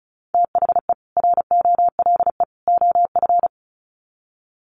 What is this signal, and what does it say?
Morse code, heard as a keyed tone, about 34 wpm, 700 Hz. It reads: THE ROLE OF
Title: Sets of 3 Words 35wpm